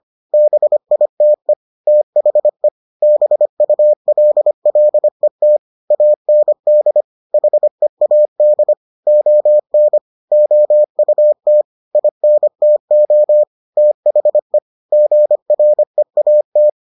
Morse code, heard as a keyed tone, about 25 wpm, 600 Hz. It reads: BITE THE BULLET AND HEAD ON OUT INTO THE GREAT